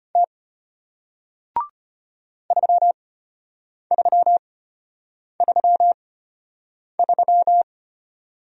Morse code, heard as a keyed tone, about 33 wpm, spaced as for 8 wpm, 700 Hz. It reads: T 3333